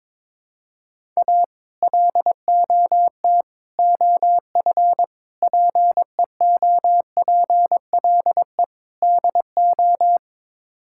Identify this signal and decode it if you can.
Morse code, heard as a keyed tone, about 22 wpm, 700 Hz. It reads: A LOT OF PEOPLE DO
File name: practice / 22wpm / Build A Phrase - 5-Word Phrases - 22wpm